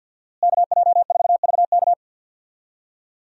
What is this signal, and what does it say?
Morse code, heard as a keyed tone, about 47 wpm, 700 Hz. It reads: KJ4VK
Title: USA Call Signs - 5 Characters - Repeated 3x - 50wpm